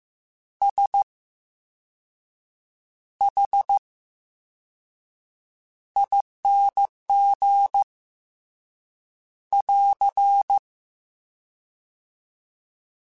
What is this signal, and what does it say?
Morse code, heard as a keyed tone, about 15 wpm, 800 Hz. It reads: S H ING <AR>
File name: Special INGs / Special INGs-mp3s / Slow ing @13